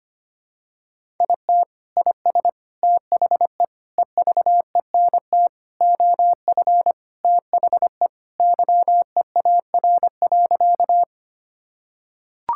Morse code, heard as a keyed tone, about 25 wpm, 700 Hz. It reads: IT IS THE EVENT OF THE YEAR.
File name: practice / 25wpm / Sentences with 3-Word Phrases - 25wpm